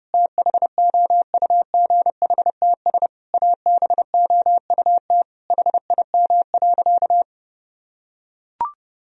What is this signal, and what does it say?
Morse code, heard as a keyed tone, about 30 wpm, 700 Hz. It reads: THOUGHTS ABOUT HIM.
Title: Sentences from Top 1000 Words - Review Part 1 - 30wpm